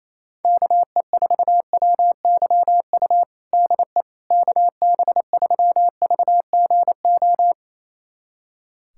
Morse code, heard as a keyed tone, about 27 wpm, 700 Hz. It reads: KE4WYU DE KB3VGO
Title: USA Callsign DE Callsign - 6 Characters - 28wpm